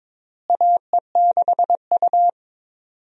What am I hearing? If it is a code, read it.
Morse code, heard as a keyed tone, about 22 wpm, 700 Hz. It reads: AE6U
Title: USA Call Signs - 4 Characters - Repeated 2x - 22wpm